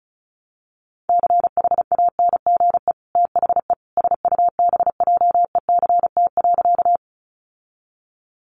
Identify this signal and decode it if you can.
Morse code, heard as a keyed tone, about 34 wpm, 700 Hz. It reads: CHANGE THE SUBJECT.
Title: Sentences from Top 800 Words 35wpm